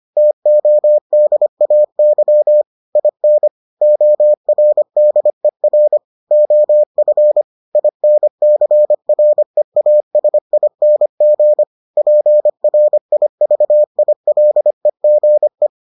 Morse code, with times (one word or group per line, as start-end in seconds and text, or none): 0.2-2.6 TODAY
3.0-3.5 IN
3.8-6.0 ORDER
6.3-7.4 OF
7.8-11.6 INCREASING
12.0-15.7 PRIVILEGE